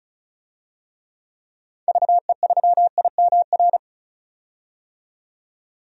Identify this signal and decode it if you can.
Morse code, heard as a keyed tone, about 36 wpm, 700 Hz. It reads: VE3IMR
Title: DXCC-100 Call Signs - 6 Characters - 35wpm